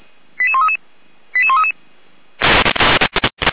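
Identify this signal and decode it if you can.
Five-tone selective call (ZVEI), heard as quick runs of five tones, tones of about 70 ms. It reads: 88133 88133
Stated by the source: ZVEI five-tone selective call received with an SDR, archived by the Feuerwehr Orlamünde fire brigade.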